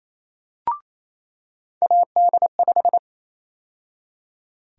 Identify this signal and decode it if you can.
Morse code, heard as a keyed tone, about 28 wpm, 700 Hz. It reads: AD5
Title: USA Call Sign Prefixes - 28wpm